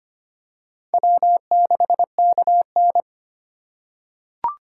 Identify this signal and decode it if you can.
Morse code, heard as a keyed tone, about 25 wpm, 700 Hz. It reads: W6KN
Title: USA Call Signs - 3x Character Spacing - 25wpm